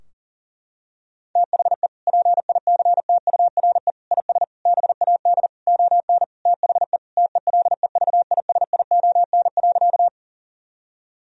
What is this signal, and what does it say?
Morse code, heard as a keyed tone, about 40 wpm, 700 Hz. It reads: THE PICTURE IS BAD ON THE TELEVISION.
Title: Sentences from Top 1000 Words - Review Part 4 - 40wpm